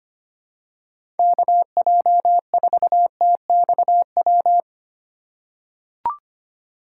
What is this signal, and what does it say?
Morse code, heard as a keyed tone, about 25 wpm, 700 Hz. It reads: KJ4TXW
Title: USA Call Signs - 6 Characters - 4x Character Spacing - 25wpm